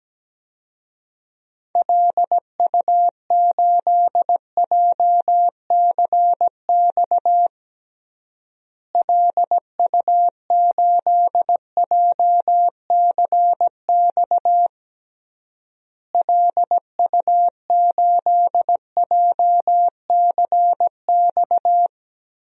Morse code, heard as a keyed tone, about 17 wpm, 700 Hz. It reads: LU8JCX LU8JCX LU8JCX
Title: DXCC-100 Call Signs - 6 Characters - Repeated 3x - 17wpm